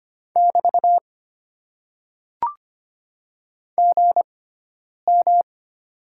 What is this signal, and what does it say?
Morse code, heard as a keyed tone, about 25 wpm, spaced as for 9 wpm, 700 Hz. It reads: <BT> GM
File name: Top 90 QSO Elements - 6x Character Spacing - 25wpm